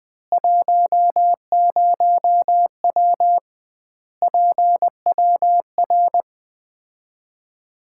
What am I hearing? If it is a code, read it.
Morse code, heard as a keyed tone, about 20 wpm, 700 Hz. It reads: 10W PWR
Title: Common QSO Phrases - 2x Word-Spacing - 20wpm